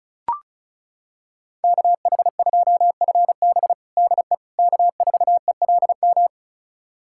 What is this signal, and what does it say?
Morse code, heard as a keyed tone, about 36 wpm, 700 Hz. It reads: KH2FB DE K4ELM
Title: USA Callsign DE Callsign - 5 Characters - 35wpm